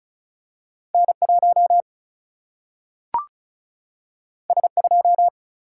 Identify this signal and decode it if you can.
Morse code, heard as a keyed tone, about 36 wpm, 700 Hz. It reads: N1 S2
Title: Single-Letter Single-Number 35wpm